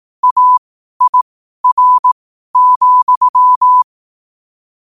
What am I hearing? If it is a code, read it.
Morse code, heard as a keyed tone, about 18 wpm, spaced as for 12 wpm, 1000 Hz. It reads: AIR,
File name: qotdfiles / 12wpm